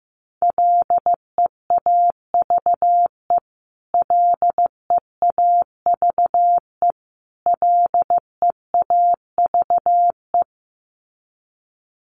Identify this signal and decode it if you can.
Morse code, heard as a keyed tone, about 15 wpm, 700 Hz. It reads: LEAVE LEAVE LEAVE
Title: Top 500 Words Repeated 15wpm